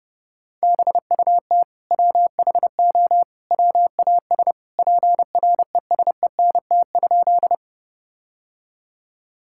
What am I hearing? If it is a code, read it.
Morse code, heard as a keyed tone, about 32 wpm, 700 Hz. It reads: BUT WHO WAS PRESENT?
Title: Sentences from Top 1000 Words - Review Part 5 - 30wpm